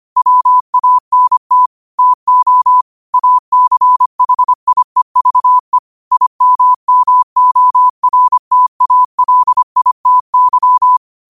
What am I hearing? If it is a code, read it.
Morse code, heard as a keyed tone, about 25 wpm, 1000 Hz. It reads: WANT TO ACHIEVE IMMORTALITY
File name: qotdfiles / 25wpm